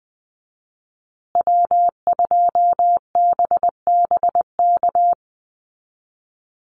Morse code, heard as a keyed tone, about 20 wpm, 700 Hz. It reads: W2BBK
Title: USA Call Signs Repeated 2x 20wpm